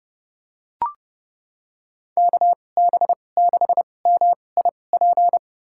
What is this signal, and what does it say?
Morse code, heard as a keyed tone, about 30 wpm, spaced as for 22 wpm, 700 Hz. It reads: KB6MIP
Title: USA Call Signs - 6 Characters - 2x Character Spacing - 30wpm